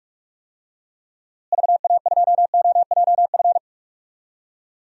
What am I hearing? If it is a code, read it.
Morse code, heard as a keyed tone, about 46 wpm, 700 Hz. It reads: VA2OJF
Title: DXCC-100 Call Signs - 6 Characters - Repeated 3x - 45wpm